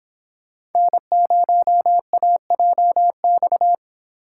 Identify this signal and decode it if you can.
Morse code, heard as a keyed tone, about 22 wpm, 700 Hz. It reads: N0AJX
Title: Speed Racing - USA 1x3 Call Signs 22wpm